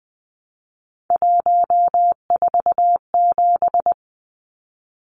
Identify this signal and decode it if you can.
Morse code, heard as a keyed tone, about 20 wpm, 700 Hz. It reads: 147
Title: Numbers - 3-Digit 20wpm